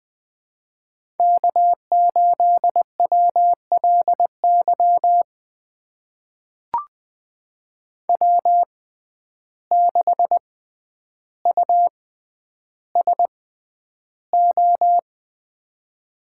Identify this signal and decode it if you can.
Morse code, heard as a keyed tone, about 20 wpm, spaced as for 10 wpm, 700 Hz. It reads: K8WLY W6USO
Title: USA Call Signs - 5 Characters - 6x Character Spacing - 20wpm